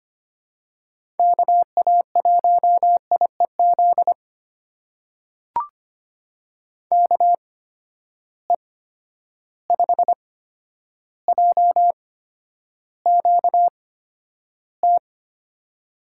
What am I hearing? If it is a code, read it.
Morse code, heard as a keyed tone, about 25 wpm, spaced as for 11 wpm, 700 Hz. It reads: KA1IEZ KE5JQT
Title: USA Call Signs - 6 Characters - 8x Character Spacing - 25wpm